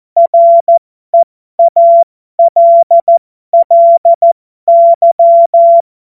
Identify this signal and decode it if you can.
Morse code, heard as a keyed tone, about 14 wpm, spaced as for 12 wpm, 680 Hz. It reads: REALLY
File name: Morsekurs / Witze